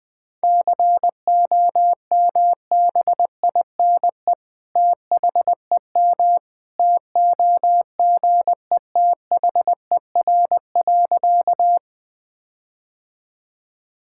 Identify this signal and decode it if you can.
Morse code, heard as a keyed tone, about 20 wpm, 700 Hz. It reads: COMBINE THEM TOGETHER.
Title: Sentences from Top 1600 Words 20wpm